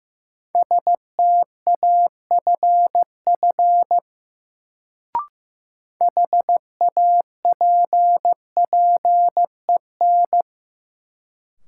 Morse code, heard as a keyed tone, about 16 wpm, 700 Hz. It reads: STAFF HAPPEN
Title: Sets of 5 Words Encore - New Word Drill - 15wpm